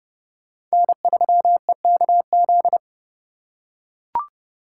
Morse code, heard as a keyed tone, about 29 wpm, 700 Hz. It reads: N3EKZ